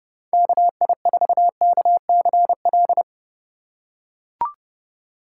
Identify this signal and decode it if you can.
Morse code, heard as a keyed tone, about 30 wpm, 700 Hz. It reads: KI4KCL